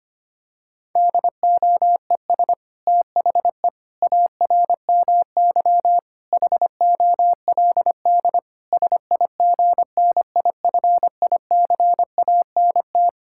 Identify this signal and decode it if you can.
Morse code, heard as a keyed tone, about 25 wpm, 700 Hz. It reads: DOES THE ARMY HOLD SIGNIFICANT
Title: Sentences from Top 1000 Words - Review Part 6 - 25wpm